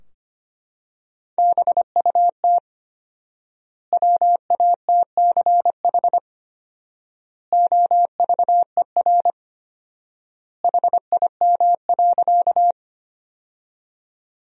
Morse code, heard as a keyed tone, about 25 wpm, 700 Hz. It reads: BUT WATCH OVER HIM.
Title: 4x Word-Spacing - Sentences from Top 300 Words - 25wpm